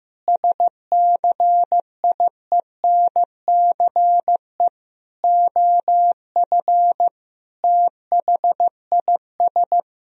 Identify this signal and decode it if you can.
Morse code, heard as a keyed tone, about 15 wpm, 700 Hz. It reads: SCIENCE OF THIS
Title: Sentences from Top 1000 Words - Review Part 2 - 15wpm